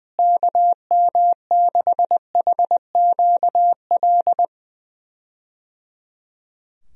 Morse code, heard as a keyed tone, about 21 wpm, 700 Hz. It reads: KM6HQL
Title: USA Call Signs - 6 Characters - Repeated 3x - 20wpm